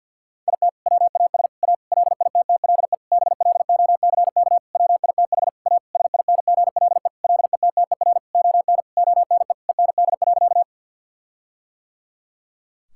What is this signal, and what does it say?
Morse code, heard as a keyed tone, about 50 wpm, 700 Hz. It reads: IT WAS A LITTLE BLOCK WITH A SINGLE LETTER ON ONE END.
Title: Sentences from Top 1600 Words 50wpm